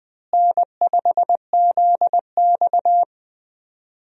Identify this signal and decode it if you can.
Morse code, heard as a keyed tone, about 21 wpm, 700 Hz. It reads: N5ZX
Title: USA Call Signs - 4 Characters - Repeated 3x - 20wpm